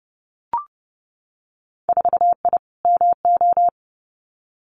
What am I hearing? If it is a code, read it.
Morse code, heard as a keyed tone, about 30 wpm, 700 Hz. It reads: 4I MO
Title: Winter Field Day Exchanges 30wpm